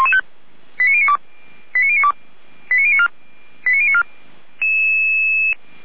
Five-tone selective call (ZVEI), heard as quick runs of five tones, tones of about 70 ms. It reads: 89092 89092 89094
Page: SDR/ZVEI